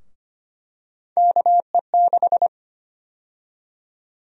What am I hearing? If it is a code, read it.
Morse code, heard as a keyed tone, about 25 wpm, 700 Hz. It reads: KE6
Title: USA Call Sign Prefixes - 25wpm